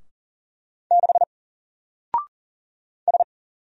Morse code, heard as a keyed tone, about 40 wpm, 700 Hz. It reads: 6 S
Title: Single Character (Letter:Number:Punctuation:Prosign) 40wpm